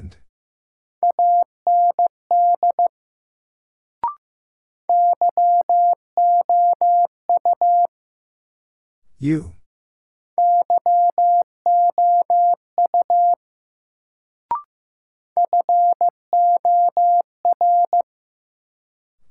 Morse code, heard as a keyed tone, about 15 wpm, 700 Hz. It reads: AND YOU YOU FOR
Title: Sets of 2 Words - New Word Drill - 15wpm